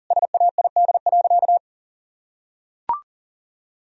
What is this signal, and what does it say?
Morse code, heard as a keyed tone, about 40 wpm, 700 Hz. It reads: SAID.
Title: Sentences from Top 1000 Words - Review Part 4 - 40wpm